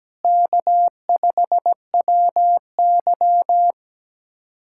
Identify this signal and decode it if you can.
Morse code, heard as a keyed tone, about 17 wpm, 700 Hz. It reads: K5WY
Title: USA Callsign DE Callsign - 4 Characters - 17wpm